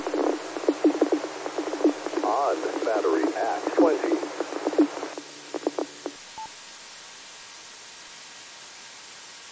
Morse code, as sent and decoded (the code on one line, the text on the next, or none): none